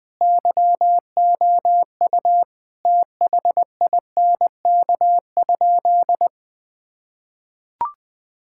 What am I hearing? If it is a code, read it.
Morse code, heard as a keyed tone, about 21 wpm, 700 Hz. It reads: YOU THINK?
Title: Sentences with 2-Word Phrases - 20wpm